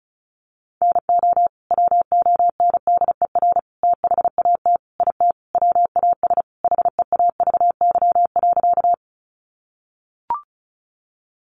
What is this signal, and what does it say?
Morse code, heard as a keyed tone, about 34 wpm, 700 Hz. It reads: NO WONDER THAT IT WAS HEAVY.
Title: Sentences from Top 1100 Words 35wpm